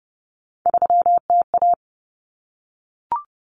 Morse code, heard as a keyed tone, about 30 wpm, 700 Hz. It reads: 3TA
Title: Common Cut Numbers - 3-Digit 30wpm